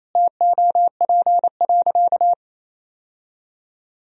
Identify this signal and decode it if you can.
Morse code, heard as a keyed tone, about 28 wpm, 700 Hz. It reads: TOP.
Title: Sentences from Top 1000 Words - Review Part 2 - 28wpm